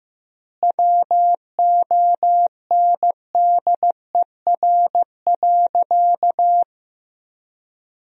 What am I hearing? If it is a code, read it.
Morse code, heard as a keyed tone, about 15 wpm, 700 Hz. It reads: WONDER.
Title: 3x Word-Spacing - Sentences from Top 500 Words - 15wpm